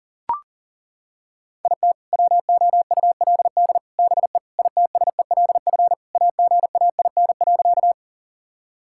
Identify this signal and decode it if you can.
Morse code, heard as a keyed tone, about 40 wpm, 700 Hz. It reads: IT WOULD BE ITSELF AGAIN.
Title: Sentences from Top 1000 Words - Review Part 5 - 40wpm